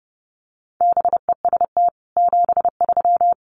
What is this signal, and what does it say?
Morse code, heard as a keyed tone, about 30 wpm, 700 Hz. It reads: BEST 73
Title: Common QSO Phrases 30wpm